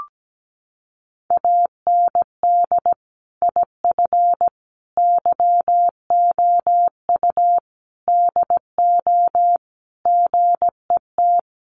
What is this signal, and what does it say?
Morse code, heard as a keyed tone, about 17 wpm, 700 Hz. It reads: AND IF YOU DO GET
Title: Sentences from Top 1500 Words 17wpm